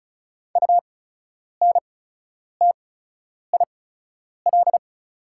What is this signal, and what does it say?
Morse code, heard as a keyed tone, about 36 wpm, spaced as for 10 wpm, 700 Hz. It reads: UNTIL
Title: 8x Character Spacing - Top 500 Words - 35wpm